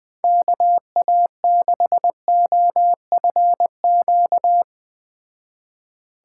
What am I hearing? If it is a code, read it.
Morse code, heard as a keyed tone, about 20 wpm, 700 Hz. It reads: KA6OFQ